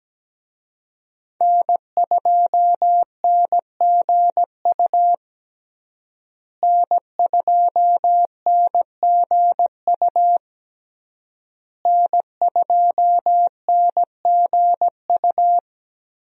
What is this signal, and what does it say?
Morse code, heard as a keyed tone, about 17 wpm, 700 Hz. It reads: N2NGU N2NGU N2NGU
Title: USA Call Signs - 5 Characters - Repeated 3x - 17wpm